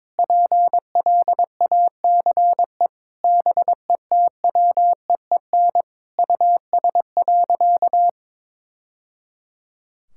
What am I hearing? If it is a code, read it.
Morse code, heard as a keyed tone, about 22 wpm, 700 Hz. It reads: PLACE BETWEEN US.
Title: Sentences from Top 1000 Words - Review Part 6 - 22wpm